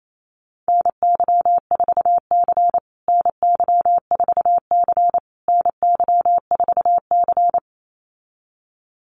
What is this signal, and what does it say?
Morse code, heard as a keyed tone, about 28 wpm, 700 Hz. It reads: NY4C NY4C NY4C
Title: USA Call Signs Repeated 3x 28wpm